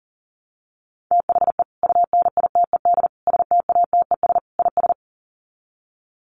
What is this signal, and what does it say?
Morse code, heard as a keyed tone, about 39 wpm, 700 Hz. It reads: THE UNITED STATES IS
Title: Sets of 4 Words 40wpm